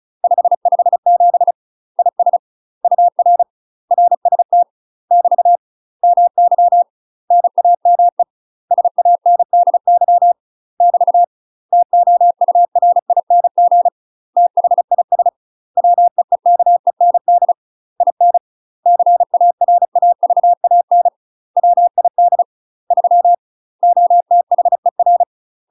Morse code, heard as a keyed tone, about 34 wpm, 700 Hz. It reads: 557 IS UR RST <BT> MY NAME SANDY <BT> TOURING THIS WEEKEND IN CARAVAN WID 3 OTHER
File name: UK MASTER QSOs TRAINING/35WPM